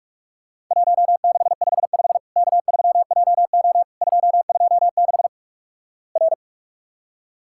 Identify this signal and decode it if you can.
Morse code, heard as a keyed tone, about 46 wpm, 700 Hz. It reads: 1655 K3JO 226 R
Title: Worked All Europe DX Contest - QTC Exchange - 45wpm